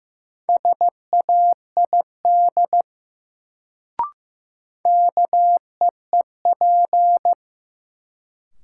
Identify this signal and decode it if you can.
Morse code, heard as a keyed tone, about 15 wpm, 700 Hz. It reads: SAID KEEP
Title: Sets of 2 Words - New Word Drill - 15wpm